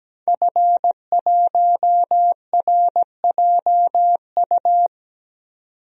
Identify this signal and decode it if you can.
Morse code, heard as a keyed tone, about 17 wpm, 700 Hz. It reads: F1RJU
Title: DXCC-100 Call Signs - 5 Characters - 17wpm